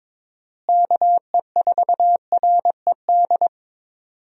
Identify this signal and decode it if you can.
Morse code, heard as a keyed tone, about 21 wpm, 700 Hz. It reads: KE4RED
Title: USA Call Signs - 6 Characters - Repeated 3x - 22wpm